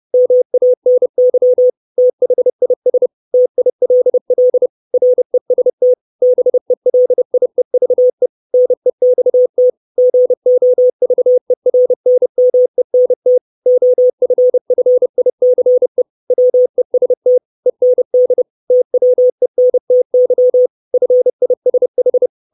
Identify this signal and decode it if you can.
Morse code, heard as a keyed tone, about 30 wpm, 500 Hz. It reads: MANY THIS TILL REST BELIEVE NEXT GOVERNMENT OFFICE WEST END TWENTY FISH
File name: 632 Words 30WPM